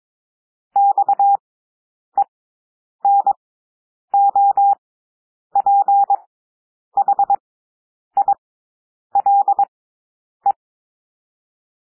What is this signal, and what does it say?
Morse code, heard as a keyed tone, about 22 wpm, spaced as for 9 wpm, 800 Hz. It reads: XENOPHILE
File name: cwmp3 / W2 SOUND WORDS 22 wpm